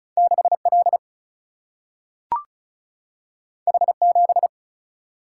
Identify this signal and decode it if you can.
Morse code, heard as a keyed tone, about 36 wpm, 700 Hz. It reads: 6L H7
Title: Single-Letter Single-Number 35wpm